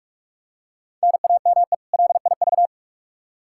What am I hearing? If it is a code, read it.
Morse code, heard as a keyed tone, about 42 wpm, 700 Hz. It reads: NAME LIV
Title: Practice QSO - Exchange 1 - Beginning - 45wpm